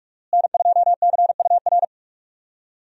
Morse code, heard as a keyed tone, about 46 wpm, 700 Hz. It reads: N2CUR
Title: USA Call Signs - 5 Characters - Repeated 3x - 45wpm